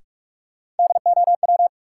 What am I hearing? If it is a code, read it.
Morse code, heard as a keyed tone, about 44 wpm, 700 Hz. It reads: DOW
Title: Sets of 5 Words - New Word Drill - 45wpm